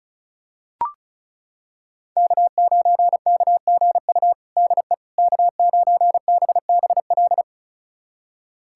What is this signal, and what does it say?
Morse code, heard as a keyed tone, about 36 wpm, 700 Hz. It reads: K9KGU DE K9BBL
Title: USA Callsign DE Callsign - 5 Characters - 35wpm